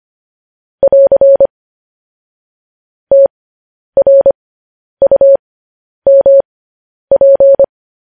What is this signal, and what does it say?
Morse code, heard as a keyed tone, about 25 wpm, spaced as for 10 wpm, 550 Hz. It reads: <AR> TRUMP